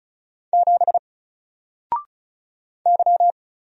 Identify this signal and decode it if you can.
Morse code, heard as a keyed tone, about 36 wpm, 700 Hz. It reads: 7 Y
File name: Single-Letter-Number-35wpm